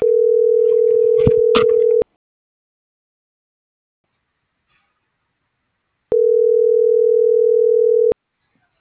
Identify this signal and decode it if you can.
Ringback tone (North America), 440 + 480 Hz.